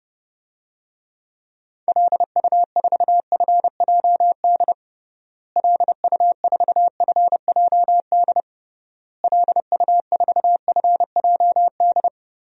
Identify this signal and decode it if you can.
Morse code, heard as a keyed tone, about 32 wpm, 700 Hz. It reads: LU4FJD LU4FJD LU4FJD
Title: DXCC-100 Call Signs - 6 Characters - Repeated 3x - 30wpm